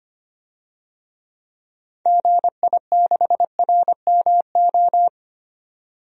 Morse code, heard as a keyed tone, about 25 wpm, 700 Hz. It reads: GI6RMO
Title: DXCC-100 Call Signs - 6 Characters - 25wpm